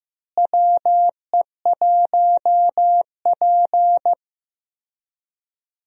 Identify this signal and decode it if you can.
Morse code, heard as a keyed tone, about 15 wpm, 700 Hz. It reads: WE1P
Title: USA Call Signs - 4 Characters - Repeated 3x - 15wpm